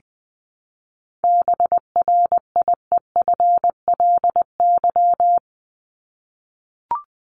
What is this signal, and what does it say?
Morse code, heard as a keyed tone, about 20 wpm, 700 Hz. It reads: BRIEFLY
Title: LY Suffix - Top 100 Words with Base Words 20wpm